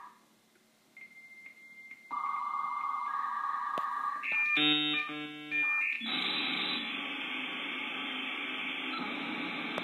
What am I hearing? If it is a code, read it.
Dial-up modem handshake.
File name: FrHalper-audio-modem56k